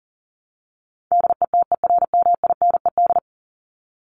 Morse code, heard as a keyed tone, about 40 wpm, 700 Hz. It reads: DETERMINED
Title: ED Suffix - Top 100 Words with Base Words 40wpm